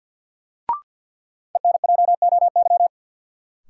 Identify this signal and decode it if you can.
Morse code, heard as a keyed tone, about 46 wpm, 700 Hz. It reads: ENJOY
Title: Sentences from Top 1000 Words - Review Part 4 - New Word Drill - 50wpm